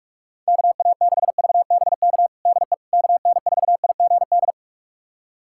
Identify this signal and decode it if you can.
Morse code, heard as a keyed tone, about 42 wpm, 690 Hz. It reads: KA6VBK DE KN4IGD
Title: USA Callsign DE Callsign 45wpm